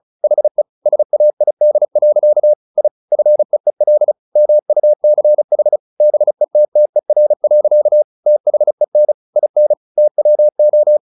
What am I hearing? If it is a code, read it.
Morse code, heard as a keyed tone, about 35 wpm, 600 Hz. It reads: HE SAID. I FEEL MUCH BETTER. THEN IN TWO
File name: the old man and the sea 2 35wpm